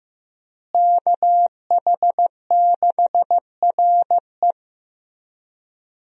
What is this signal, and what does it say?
Morse code, heard as a keyed tone, about 15 wpm, 700 Hz. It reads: KH6RE
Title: USA Call Signs - 5 Characters - Repeated 2x - 15wpm